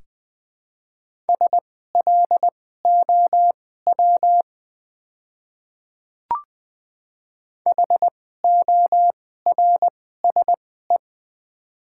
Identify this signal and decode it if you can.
Morse code, heard as a keyed tone, about 20 wpm, spaced as for 14 wpm, 700 Hz. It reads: SLOW HORSE
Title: 2x Character Spacing - Top 500 Words - 20wpm